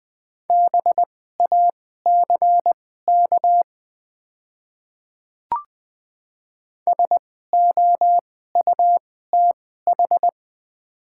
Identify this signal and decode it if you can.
Morse code, heard as a keyed tone, about 20 wpm, spaced as for 14 wpm, 690 Hz. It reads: BACK SOUTH
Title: 2x Character Spacing - Top 500 Words - 20wpm